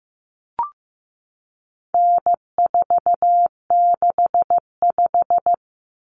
Morse code, heard as a keyed tone, about 15 wpm, 700 Hz. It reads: N465
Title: Common Cut Numbers - 4-Digit 15wpm